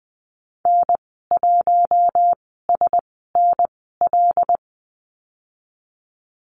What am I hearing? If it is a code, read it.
Morse code, heard as a keyed tone, about 20 wpm, spaced as for 15 wpm, 700 Hz. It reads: N1SNL
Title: 088 Lesson 15g - Number 5 - Call Signs 20wpm